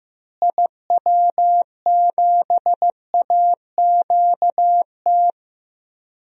Morse code, heard as a keyed tone, about 15 wpm, 700 Hz. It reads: IW7AQT
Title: DXCC-100 Call Signs - 6 Characters - Repeated 2x - 15wpm